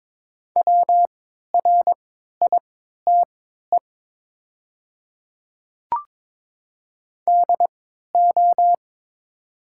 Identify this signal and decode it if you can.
Morse code, heard as a keyed tone, about 22 wpm, spaced as for 12 wpm, 700 Hz. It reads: WRITE DO
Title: 3x Character Spacing - Top 100 Words - 22wpm